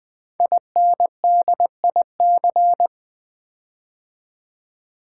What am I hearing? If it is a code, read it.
Morse code, heard as a keyed tone, about 20 wpm, 700 Hz. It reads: INDIC
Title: Build A Word - 5 Letters At A Time - 10-Letter Words - 20wpm